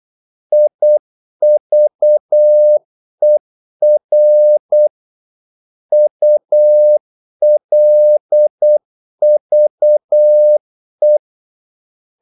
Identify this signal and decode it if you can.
Morse code, heard as a keyed tone, about 8 wpm, 600 Hz. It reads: IVER ULVE
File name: Korte ord DK 8wpm